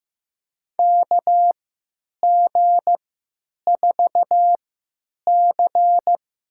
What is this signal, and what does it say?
Morse code, heard as a keyed tone, about 15 wpm, spaced as for 9 wpm, 700 Hz. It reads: KG4C
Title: USA Call Signs - 3x Character Spacing - 15wpm